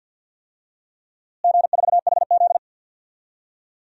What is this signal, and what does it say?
Morse code, heard as a keyed tone, about 49 wpm, 700 Hz. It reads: G4HZ
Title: DXCC-100 Call Signs - 4 Characters - Repeated 3x - 50wpm